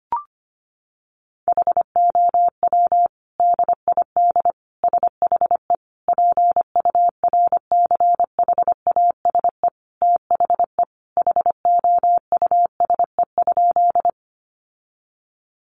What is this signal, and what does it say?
Morse code, heard as a keyed tone, about 25 wpm, 700 Hz. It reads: HOW DID SHE PURCHASE THE HOUSE?
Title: Sentences from Top 800 Words 25wpm